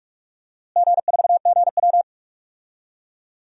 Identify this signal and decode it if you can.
Morse code, heard as a keyed tone, about 42 wpm, 700 Hz. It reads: G4GW
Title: DXCC-100 Call Signs - 4 Characters - Repeated 3x - 45wpm